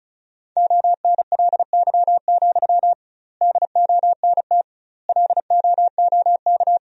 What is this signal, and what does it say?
Morse code, heard as a keyed tone, about 36 wpm, 700 Hz. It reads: ONLY, DONT LOOK
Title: 2x Word-Spacing - Sentences from Top 300 Words - 35wpm